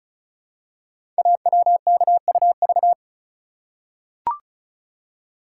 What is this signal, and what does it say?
Morse code, heard as a keyed tone, about 36 wpm, 700 Hz. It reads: AWKUV
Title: Code Groups - 5 Letters - 35wpm